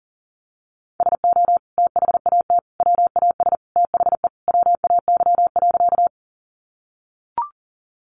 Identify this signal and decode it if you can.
Morse code, heard as a keyed tone, about 40 wpm, 700 Hz. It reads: SO THAT WAS THE WAY.
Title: Sentences from Top 100 Words 40wpm